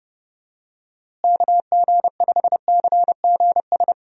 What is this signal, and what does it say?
Morse code, heard as a keyed tone, about 30 wpm, 700 Hz. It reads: KG5CGS
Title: USA Call Signs - 6 Characters - Repeated 3x - 30wpm